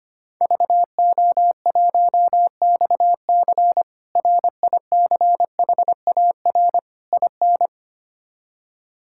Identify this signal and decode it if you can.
Morse code, heard as a keyed tone, about 25 wpm, 700 Hz. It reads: VO1XC RICHAR IN
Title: K1USN SST QSO Practice - 25wpm